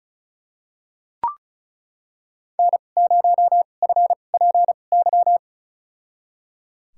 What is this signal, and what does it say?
Morse code, heard as a keyed tone, about 36 wpm, spaced as for 25 wpm, 700 Hz. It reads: N0FPY
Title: USA Call Signs - 2x Character Spacing - 35wpm